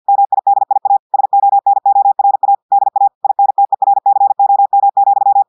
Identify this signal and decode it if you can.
Morse code, heard as a keyed tone, about 50 wpm, 800 Hz. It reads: MIDIA SONORA DA INTERCOM,
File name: audios / portugais / portugais texte en clair 1 vitesse 50 mots minutes